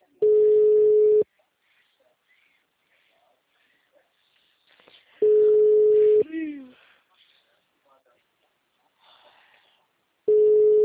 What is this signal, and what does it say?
Ringback tone (Europe, 425 Hz), with speech.